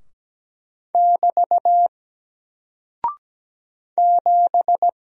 Morse code, heard as a keyed tone, about 17 wpm, 700 Hz. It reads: <BT> 7
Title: Single Character (Letter:Number:Punctuation:Prosign) 17wpm